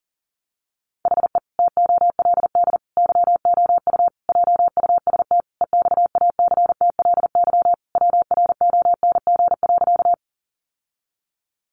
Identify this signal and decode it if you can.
Morse code, heard as a keyed tone, about 40 wpm, 700 Hz. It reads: HE TOLD YOU JUST EXACTLY WRONG.
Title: Sentences from Top 900 Words 40wpm